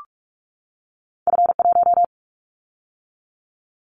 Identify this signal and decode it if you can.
Morse code, heard as a keyed tone, about 46 wpm, 700 Hz. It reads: F1
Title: Callsign Prefixes for Top 50 DXCC - 45wpm